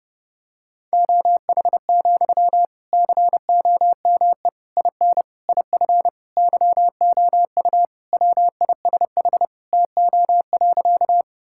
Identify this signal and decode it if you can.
Morse code, heard as a keyed tone, about 29 wpm, 700 Hz. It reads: OH, COME IN IF YOU WISH TO.
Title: Sentences from Top 1600 Words 30wpm